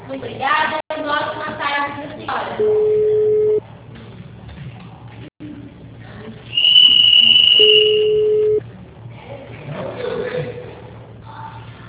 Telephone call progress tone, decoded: ringback tone

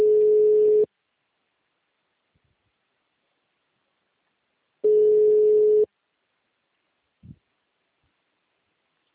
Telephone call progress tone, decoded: ringback tone

